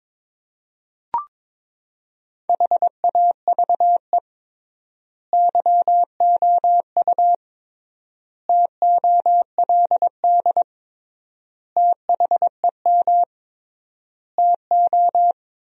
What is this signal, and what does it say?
Morse code, heard as a keyed tone, about 22 wpm, 700 Hz. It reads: HAVE YOU TOLD THEM TO
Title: 3x Word-Spacing - Sentences from Top 500 Words - 22wpm